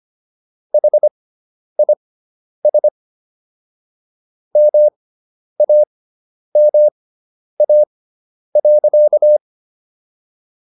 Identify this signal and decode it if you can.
Morse code, heard as a keyed tone, about 25 wpm, spaced as for 10 wpm, 600 Hz. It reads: HIS MAMA.